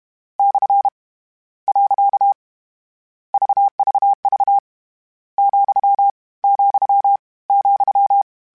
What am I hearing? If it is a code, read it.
Morse code, heard as a keyed tone, about 32 wpm, 800 Hz. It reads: / . VVV , , ,
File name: Misc / Misc-mp3s / Punct @30